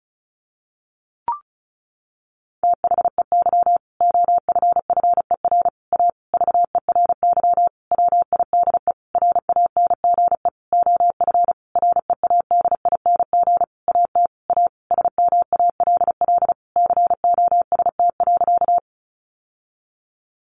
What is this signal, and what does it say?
Morse code, heard as a keyed tone, about 34 wpm, 700 Hz. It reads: THEY OFFER A VERY WIDE RANGE OF READING AT A SMALL COST.
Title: Sentences from Top 900 Words 35wpm